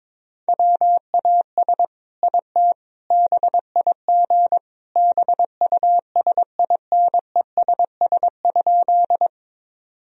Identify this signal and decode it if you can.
Morse code, heard as a keyed tone, about 22 wpm, 700 Hz. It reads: WAS IT BIG BUSINESS?
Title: Sentences from Top 1000 Words - Review Part 1 - 22wpm